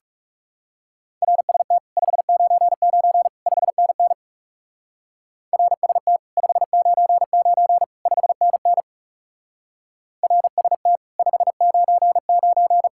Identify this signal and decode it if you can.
Morse code, heard as a keyed tone, about 37 wpm, 700 Hz. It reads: RST 599 5NN RST 599 5NN RST 599
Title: Speed-Racing Common QSO Phrases - 30wpm